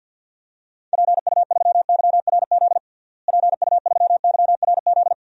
Morse code, heard as a keyed tone, about 48 wpm, 700 Hz. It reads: PU3YRZ PU3YRZ
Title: DXCC-100 Call Signs - 6 Characters - Repeated 3x - 50wpm